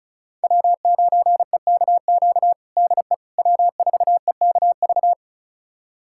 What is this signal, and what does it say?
Morse code, heard as a keyed tone, about 34 wpm, 700 Hz. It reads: W9EKQ DE W4EKV
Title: USA Callsign DE Callsign - 5 Characters - 35wpm